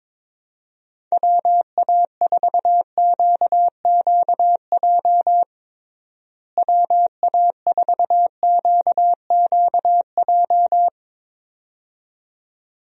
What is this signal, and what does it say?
Morse code, heard as a keyed tone, about 22 wpm, 700 Hz. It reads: WA4QQJ WA4QQJ
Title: USA Call Signs - 6 Characters - Repeated 2x - 22wpm